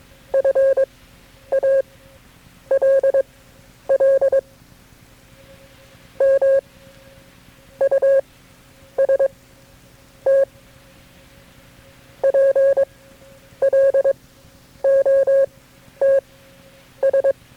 Morse code, heard as a keyed tone, about 22 wpm, spaced as for 9 wpm, 540 Hz. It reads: FALL MUST PLOTS
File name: FreeFiles / CW-practice / KB6MT-tape3-side1